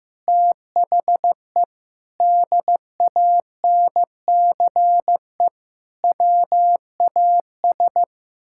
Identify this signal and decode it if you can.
Morse code, heard as a keyed tone, about 15 wpm, 700 Hz. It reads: THE DANCE WAS
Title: Sentences from Top 1000 Words - Review Part 6 - 15wpm